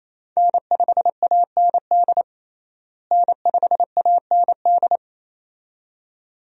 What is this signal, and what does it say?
Morse code, heard as a keyed tone, about 28 wpm, 700 Hz. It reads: N5AND N5AND
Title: USA Call Signs - 5 Characters - Repeated 2x - 28wpm